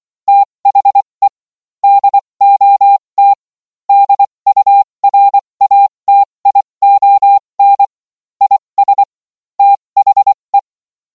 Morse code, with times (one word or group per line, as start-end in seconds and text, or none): 0.3-1.3 THE
1.8-3.3 DOT
3.9-7.8 DURATION
8.4-9.0 IS
9.6-10.6 THE